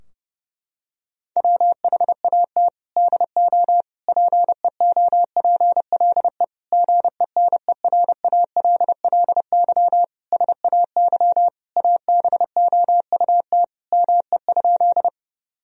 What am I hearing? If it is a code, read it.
Morse code, heard as a keyed tone, about 32 wpm, 700 Hz. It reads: WHAT DO PEOPLE GENERALLY SAY ABOUT ME?
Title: Sentences from Top 1000 Words - Review Part 4 - 30wpm